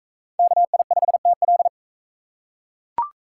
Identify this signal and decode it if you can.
Morse code, heard as a keyed tone, about 39 wpm, 700 Hz. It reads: KI5TL